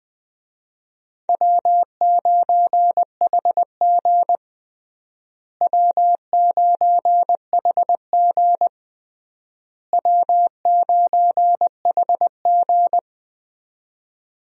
Morse code, heard as a keyed tone, about 20 wpm, 700 Hz. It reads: W9HG W9HG W9HG
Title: USA Call Signs - 4 Characters - Repeated 3x - 20wpm